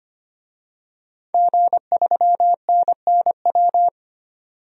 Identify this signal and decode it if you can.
Morse code, heard as a keyed tone, about 25 wpm, 700 Hz. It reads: G3NNW
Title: DXCC-100 Call Signs - 5 Characters - 25wpm